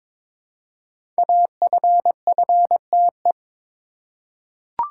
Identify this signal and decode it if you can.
Morse code, heard as a keyed tone, about 22 wpm, 700 Hz. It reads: AFFTE
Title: Code Groups - 5 Letters - 22wpm